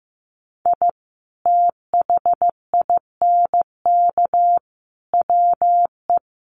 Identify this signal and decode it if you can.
Morse code, heard as a keyed tone, about 15 wpm, 700 Hz. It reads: I THINK WE
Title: Sets of 4 Words 15wpm